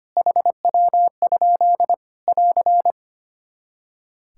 Morse code, heard as a keyed tone, about 25 wpm, 700 Hz. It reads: HW? <AR>
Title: K1USN SST QSO Practice 25wpm